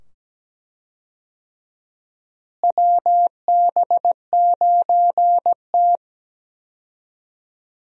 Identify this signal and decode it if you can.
Morse code, heard as a keyed tone, about 17 wpm, 700 Hz. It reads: WB9T
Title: USA Call Signs - 4 Characters - Repeated 2x - 17wpm